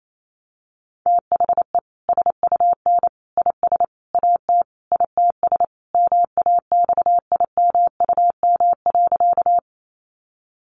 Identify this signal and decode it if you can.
Morse code, heard as a keyed tone, about 28 wpm, 700 Hz. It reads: THE SUN IS AT ITS MAXIMUM.
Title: Sentences from Top 1300 Words 28wpm